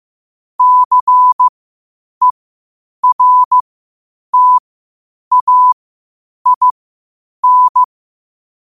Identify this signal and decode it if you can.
Morse code, heard as a keyed tone, about 15 wpm, spaced as for 8 wpm, 1000 Hz. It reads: CERTAIN
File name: qotdfiles / 8wpm